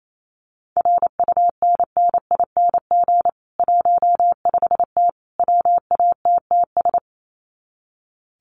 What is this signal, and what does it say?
Morse code, heard as a keyed tone, about 28 wpm, 700 Hz. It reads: RUNNING 15T WATTS
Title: Common QSO Phrases 28wpm